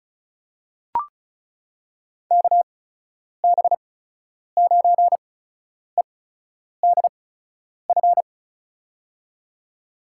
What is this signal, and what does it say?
Morse code, heard as a keyed tone, about 36 wpm, spaced as for 10 wpm, 700 Hz. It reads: KB9EDF
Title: USA Call Signs - 6 Characters - 8x Character Spacing - 35wpm